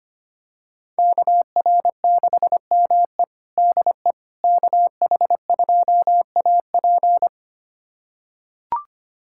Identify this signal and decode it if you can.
Morse code, heard as a keyed tone, about 25 wpm, 700 Hz. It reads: KR6ME DE KH2AP
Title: USA Callsign DE Callsign - 5 Characters - 25wpm